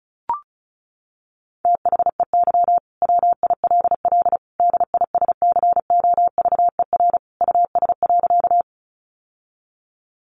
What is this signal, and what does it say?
Morse code, heard as a keyed tone, about 34 wpm, 700 Hz. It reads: THEY WILL DISCOVER US.
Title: Sentences from Top 1100 Words 35wpm